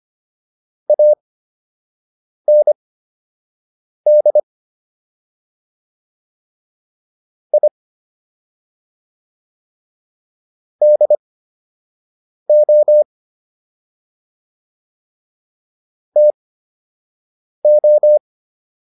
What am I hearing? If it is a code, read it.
Morse code, heard as a keyed tone, about 25 wpm, spaced as for 6 wpm, 600 Hz. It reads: AND I DO TO